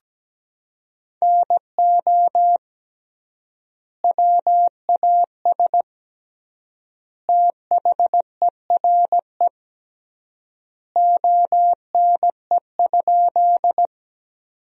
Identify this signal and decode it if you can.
Morse code, heard as a keyed tone, about 17 wpm, 700 Hz. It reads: NO WAS THERE ONE?
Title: 3x Word-Spacing - Sentences from Top 100 Words - 17wpm